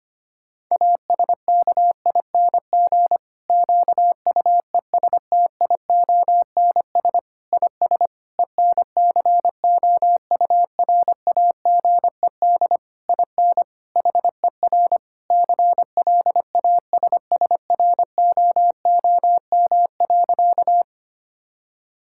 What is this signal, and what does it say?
Morse code, heard as a keyed tone, about 25 wpm, 700 Hz. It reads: ASKING QUESTIONS IS ENCOURAGED IN HER CLASSROOM.
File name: practice / 25wpm / Difficult Sentences with Hints - 25wpm